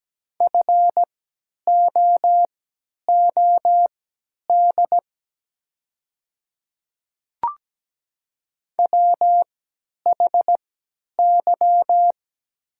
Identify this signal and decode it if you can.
Morse code, heard as a keyed tone, about 17 wpm, spaced as for 10 wpm, 700 Hz. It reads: FOOD WHY
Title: 3x Character Spacing - Top 500 Words - 17wpm